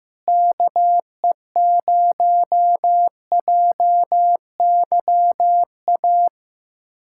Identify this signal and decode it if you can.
Morse code, heard as a keyed tone, about 15 wpm, 700 Hz. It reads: KE0JYA
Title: USA Callsign DE Callsign 15wpm